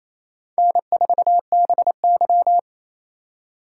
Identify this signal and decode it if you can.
Morse code, heard as a keyed tone, about 34 wpm, 700 Hz. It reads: N4BY
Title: Speed Racing - USA 1x2 Call Signs 28wpm